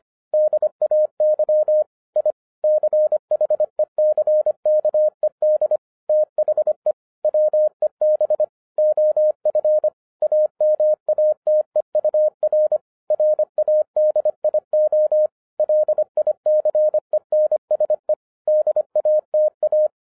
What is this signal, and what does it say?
Morse code, heard as a keyed tone, about 25 wpm, 610 Hz. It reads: DAY I CHECKED THE WEB OF AMATEUR RADIO LICENSE DATA
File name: 2012-01-mp3 / n08-ITS MY FAULT, BUT-25wpm